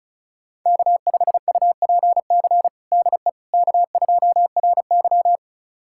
Morse code, heard as a keyed tone, about 36 wpm, 700 Hz. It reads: K5UPC DE K2RY
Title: USA Callsign DE Callsign 35wpm